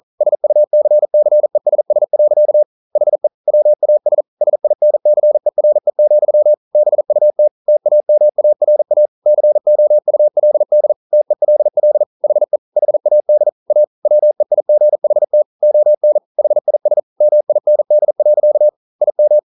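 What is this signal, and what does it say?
Morse code, heard as a keyed tone, about 42 wpm, 600 Hz. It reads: SUCCESS. HE WAS SINCERE, BUT TAMARA COULD TELL HE HAD A WEIGHT ON HIS MIND. IM
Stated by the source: Morse code practice at 41 wpm, from CW med Gnister.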